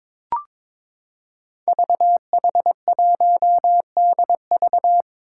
Morse code, heard as a keyed tone, about 22 wpm, 700 Hz. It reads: VH1DV
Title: Five Challenging Letter-Number Combinations 22wpm